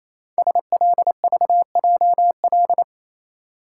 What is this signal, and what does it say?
Morse code, heard as a keyed tone, about 27 wpm, 700 Hz. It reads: SLVJL
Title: Code Groups - 5 Letters - 28wpm